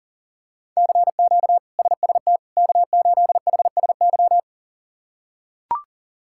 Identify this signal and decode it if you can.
Morse code, heard as a keyed tone, about 40 wpm, 700 Hz. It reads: CQ SST K8HSY
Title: K1USN SST QSO Practice - 40wpm